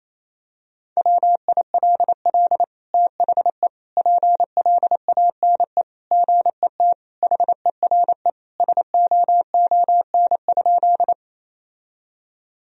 Morse code, heard as a keyed tone, about 28 wpm, 700 Hz. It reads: WILL THE PLANE GET HERE SOON?
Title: Sentences from Top 1000 Words - Review Part 3 - 28wpm